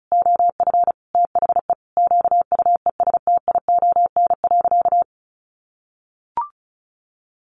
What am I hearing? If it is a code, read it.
Morse code, heard as a keyed tone, about 34 wpm, 700 Hz. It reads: OF THE QUESTION.
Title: Sentences from Top 1500 Words 35wpm